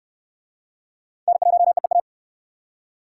Morse code, heard as a keyed tone, about 68 wpm, 700 Hz. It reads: N1EEA